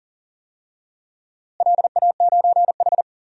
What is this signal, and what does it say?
Morse code, heard as a keyed tone, about 40 wpm, 700 Hz. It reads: LA9H
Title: DXCC-100 Call Signs - 4 Characters - Repeated 3x - 40wpm